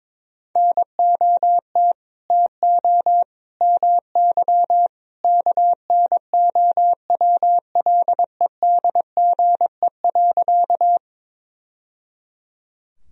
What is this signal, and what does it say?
Morse code, heard as a keyed tone, about 22 wpm, 700 Hz. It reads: NOT TO MY KNOWLEDGE.
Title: Sentences from Top 1000 Words - Review Part 3 - 22wpm